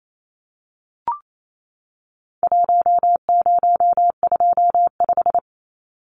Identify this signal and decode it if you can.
Morse code, heard as a keyed tone, about 28 wpm, 700 Hz. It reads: 1025
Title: Numbers - Contest Style - 4-Digit 28wpm